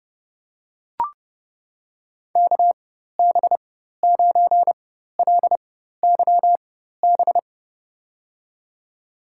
Morse code, heard as a keyed tone, about 29 wpm, spaced as for 14 wpm, 690 Hz. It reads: KB9LYB
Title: USA Call Signs - 4x Character Spacing - 30wpm